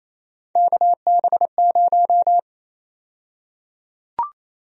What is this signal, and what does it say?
Morse code, heard as a keyed tone, about 28 wpm, 700 Hz. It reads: KB0
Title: USA Call Sign Prefixes - 28wpm